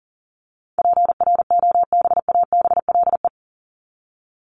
Morse code, heard as a keyed tone, about 40 wpm, 700 Hz. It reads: PROBABLE
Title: LY Suffix - Top 100 Words with Base Words 40wpm